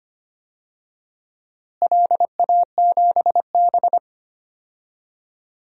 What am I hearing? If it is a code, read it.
Morse code, heard as a keyed tone, about 25 wpm, 700 Hz. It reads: LA7B
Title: DXCC-100 Call Signs - 4 Characters - Repeated 2x - 25wpm